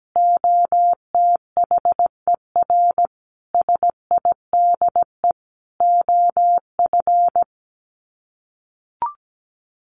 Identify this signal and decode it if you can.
Morse code, heard as a keyed tone, about 17 wpm, 700 Hz. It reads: OTHER SIDE OF
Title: Sets of 5 Words Encore 17wpm